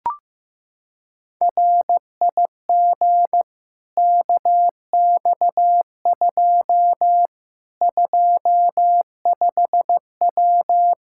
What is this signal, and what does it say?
Morse code, heard as a keyed tone, about 15 wpm, 700 Hz. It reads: RIG KX2 25W
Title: Practice QSO - Exchange 2 - Info - 15wpm